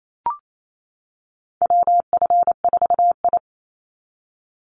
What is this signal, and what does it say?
Morse code, heard as a keyed tone, about 28 wpm, 700 Hz. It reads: WF4I
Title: NA SOTA Chaser Callsigns 28wpm